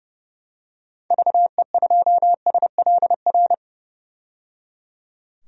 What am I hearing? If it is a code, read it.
Morse code, heard as a keyed tone, about 29 wpm, 690 Hz. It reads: VE2SLR
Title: DXCC-100 Call Signs - 6 Characters - 30wpm